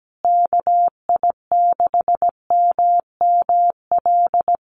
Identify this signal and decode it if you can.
Morse code, heard as a keyed tone, about 17 wpm, 700 Hz. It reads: KI6MML
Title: USA Call Signs Repeated 3x 17wpm